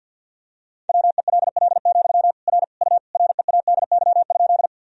Morse code, heard as a keyed tone, about 51 wpm, 700 Hz. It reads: WELL, R U READY?
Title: Sentences from Top 400 Words and Abbreviations 50wpm